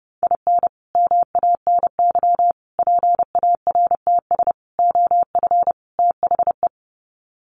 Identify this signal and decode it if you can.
Morse code, heard as a keyed tone, about 30 wpm, 700 Hz. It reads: IN MANY PARTS OF THE
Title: Sets of 5 Words Encore 30wpm